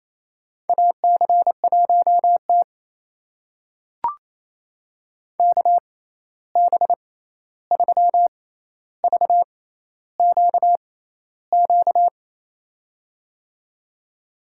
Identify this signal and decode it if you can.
Morse code, heard as a keyed tone, about 28 wpm, spaced as for 13 wpm, 700 Hz. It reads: AC1T KB3VQQ
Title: USA Call Signs - 6x Character Spacing - 28wpm